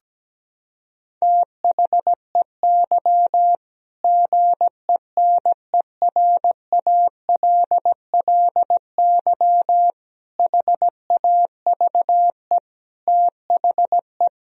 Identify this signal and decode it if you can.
Morse code, heard as a keyed tone, about 17 wpm, 700 Hz. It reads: THEY GENERALLY HAVE THE
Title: Sentences from Top 1000 Words - Review Part 4 - 17wpm